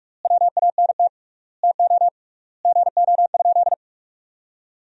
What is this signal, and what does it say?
Morse code, heard as a keyed tone, about 43 wpm, 700 Hz. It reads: WANT TO GO?
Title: 3x Word-Spacing - Sentences from Top 300 Words - 45wpm